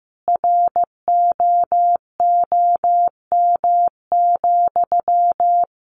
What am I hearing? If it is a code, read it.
Morse code, heard as a keyed tone, about 15 wpm, 700 Hz. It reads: ROOM,
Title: Sentences from Top 1400 Words 15wpm